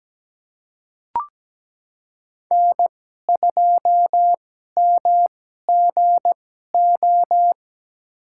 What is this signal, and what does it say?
Morse code, heard as a keyed tone, about 17 wpm, spaced as for 12 wpm, 700 Hz. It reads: N2MGO
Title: USA Call Signs - 5 Characters - 2x Character Spacing - 17wpm